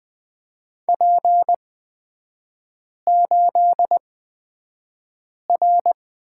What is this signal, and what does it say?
Morse code, heard as a keyed tone, about 20 wpm, spaced as for 5 wpm, 690 Hz. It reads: P8R
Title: 229 Lesson 37e - Letter x - Three Characters 20wpm